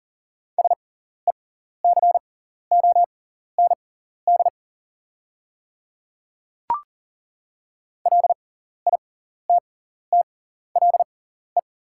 Morse code, heard as a keyed tone, about 40 wpm, spaced as for 14 wpm, 690 Hz. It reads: SECOND LITTLE
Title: 6x Character Spacing - Top 300 Words - 40wpm